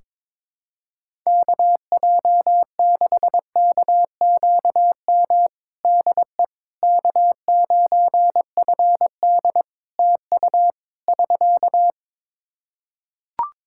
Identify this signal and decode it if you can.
Morse code, heard as a keyed tone, about 22 wpm, 700 Hz. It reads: KJ6KQM DE K9FD TU <SK>
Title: K1USN SST QSO Practice 22wpm